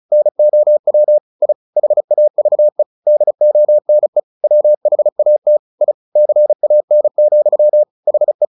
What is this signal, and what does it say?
Morse code, heard as a keyed tone, about 34 wpm, 600 Hz. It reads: NOW I HAVE DONE WHAT I CAN, HE
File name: the old man and the sea 2 35wpm